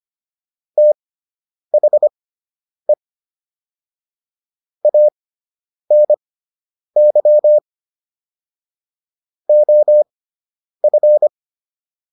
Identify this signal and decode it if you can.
Morse code, heard as a keyed tone, about 25 wpm, spaced as for 9 wpm, 600 Hz. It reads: THE ANY OF